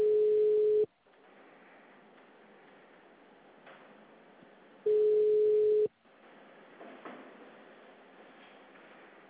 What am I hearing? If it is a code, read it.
Ringback tone (Europe, 425 Hz).
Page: Aracatuba - SP